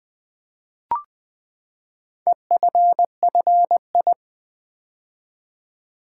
Morse code, heard as a keyed tone, about 20 wpm, 700 Hz. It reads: EFFI
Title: Build A Word - 4 Letters At A Time - 11-Letter Words - 20wpm